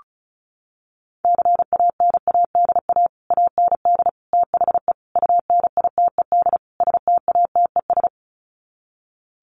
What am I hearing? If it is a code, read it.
Morse code, heard as a keyed tone, about 34 wpm, 700 Hz. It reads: CANADA AND THE UNITED STATES
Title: Sets of 5 Words Encore 35wpm